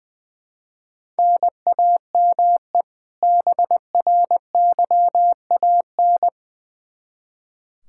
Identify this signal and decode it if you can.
Morse code, heard as a keyed tone, about 20 wpm, 700 Hz. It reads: NAME BRYAN
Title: K1USN SST QSO Practice 20wpm